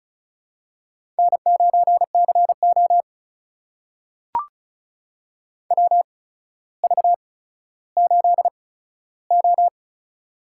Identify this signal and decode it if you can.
Morse code, heard as a keyed tone, about 36 wpm, spaced as for 15 wpm, 700 Hz. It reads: N9CO WV8O
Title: USA Call Signs - 4 Characters - 8x Character Spacing - 35wpm